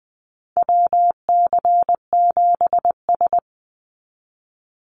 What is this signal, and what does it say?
Morse code, heard as a keyed tone, about 20 wpm, 700 Hz. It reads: WC7S
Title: Call Signs - 20wpm